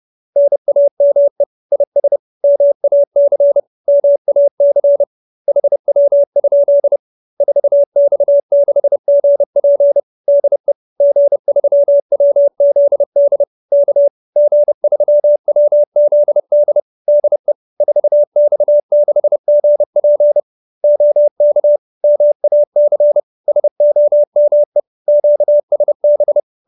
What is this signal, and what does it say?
Morse code, heard as a keyed tone, about 30 wpm, 570 Hz. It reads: NAME IS MAC MAC HW? 4X6GP DE G3WZD K G3WZD DE 4X6GP OK MAC SOME QSB
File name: QSO310 30